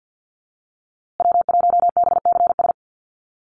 Morse code, heard as a keyed tone, about 50 wpm, 700 Hz. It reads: W1BCS
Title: USA Call Signs Repeated 3x 50wpm